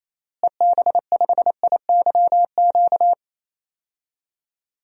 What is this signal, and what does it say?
Morse code, heard as a keyed tone, about 27 wpm, 700 Hz. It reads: EB5IYQ
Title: DXCC-100 Call Signs - 6 Characters - Repeated 2x - 28wpm